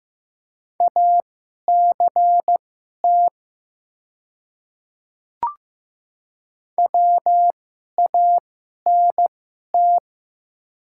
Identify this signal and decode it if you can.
Morse code, heard as a keyed tone, about 15 wpm, spaced as for 11 wpm, 700 Hz. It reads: ACT WANT
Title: 2x Character Spacing - Top 300 Words - 15wpm